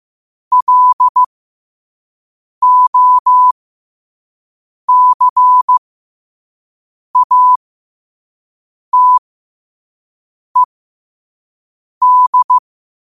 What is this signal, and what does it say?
Morse code, heard as a keyed tone, about 15 wpm, spaced as for 5 wpm, 1000 Hz. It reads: LOCATED
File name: qotdfiles / 5wpm